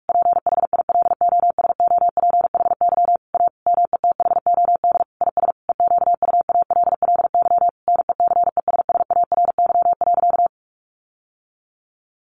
Morse code, heard as a keyed tone, about 44 wpm, 700 Hz. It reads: PHILOSOPHY A METHOD IS EQUALLY NECESSARY.
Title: Sentences from Top 1400 Words 45wpm